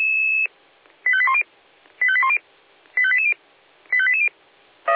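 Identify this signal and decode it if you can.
Five-tone selective call (ZVEI), heard as quick runs of five tones, tones of about 80 ms. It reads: 85810 85810 85880 85880
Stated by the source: ZVEI five-tone call received on SDR.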